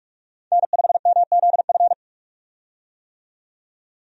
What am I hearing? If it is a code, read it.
Morse code, heard as a keyed tone, about 46 wpm, 700 Hz. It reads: N5MZF
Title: USA Call Signs - 5 Characters - 45wpm